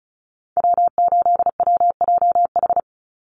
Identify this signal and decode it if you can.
Morse code, heard as a keyed tone, about 34 wpm, 700 Hz. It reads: W8WJH
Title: USA Call Signs 35wpm